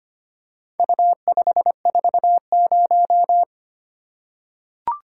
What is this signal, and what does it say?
Morse code, heard as a keyed tone, about 25 wpm, 700 Hz. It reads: U540
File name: practice / 25wpm / Four Letter-Number Combinations - 25wpm